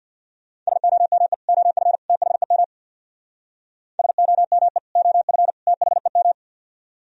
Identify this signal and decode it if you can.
Morse code, heard as a keyed tone, about 62 wpm, 700 Hz. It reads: SOME OF THEM SOME OF THEM
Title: Speed-Racing - Sets of 3 Words 50wpm